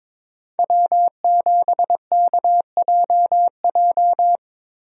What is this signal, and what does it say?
Morse code, heard as a keyed tone, about 22 wpm, 700 Hz. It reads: W7KJJ
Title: USA Call Signs - 5 Characters - Repeated 2x - 22wpm